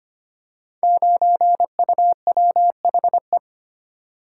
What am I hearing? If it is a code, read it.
Morse code, heard as a keyed tone, about 25 wpm, 700 Hz. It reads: 9UWHE
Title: Code Groups - 5 Letter:Numbers - 25wpm